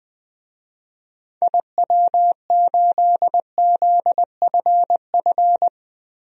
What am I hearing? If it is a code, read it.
Morse code, heard as a keyed tone, about 20 wpm, 700 Hz. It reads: IW8ZFF
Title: DXCC-100 Call Signs - 6 Characters - 20wpm